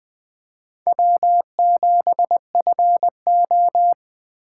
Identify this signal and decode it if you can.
Morse code, heard as a keyed tone, about 20 wpm, 700 Hz. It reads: W7FO
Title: Speed Racing - USA 1x2 Call Signs 17wpm